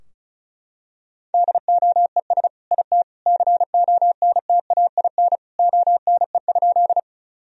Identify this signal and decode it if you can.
Morse code, heard as a keyed tone, about 34 wpm, 700 Hz. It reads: DOES IT CONTAIN ONE?
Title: Sentences from Top 1000 Words - Review Part 2 - 35wpm